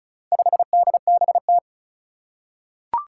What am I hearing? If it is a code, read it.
Morse code, heard as a keyed tone, about 36 wpm, 700 Hz. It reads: 5DBT
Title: Four Challenging Letter-Number Combinations 35wpm